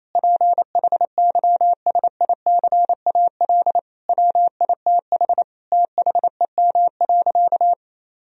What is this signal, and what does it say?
Morse code, heard as a keyed tone, about 28 wpm, 700 Hz. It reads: PHYSICAL WITH THEM.
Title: Sentences from Top 1000 Words - Review Part 4 - 28wpm